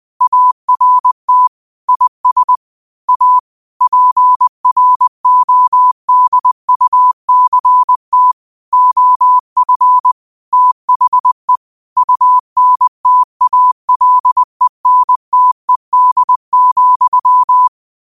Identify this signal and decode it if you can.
Morse code, heard as a keyed tone, about 20 wpm, 1000 Hz. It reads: ART IS A PRODUCT OF THE UNTALENTED,